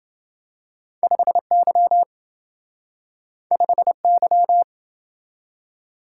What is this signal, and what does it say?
Morse code, heard as a keyed tone, about 28 wpm, 700 Hz. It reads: 5Y 5Y